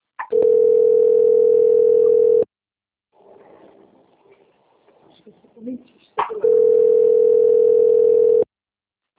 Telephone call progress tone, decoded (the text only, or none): ringback tone